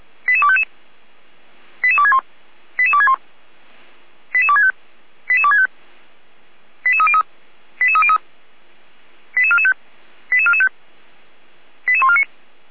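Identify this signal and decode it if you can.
Five-tone selective call (ZVEI), heard as quick runs of five tones, tones of about 70 ms. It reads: 88255 88261 88261 88265 88265 88303 88303 88445 88445 88149
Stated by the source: ZVEI five-tone selective call received on an SDR, from feuerwehr-orlamuende fire brigade.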